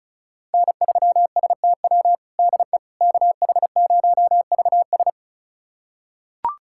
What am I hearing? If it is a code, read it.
Morse code, heard as a keyed tone, about 36 wpm, 700 Hz. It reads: N3STW DE KH0VS
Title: USA Callsign DE Callsign - 5 Characters - 35wpm